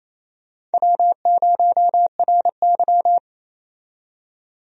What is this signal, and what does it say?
Morse code, heard as a keyed tone, about 28 wpm, 700 Hz. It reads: W0RY
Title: USA Call Signs - 4 Characters - 28wpm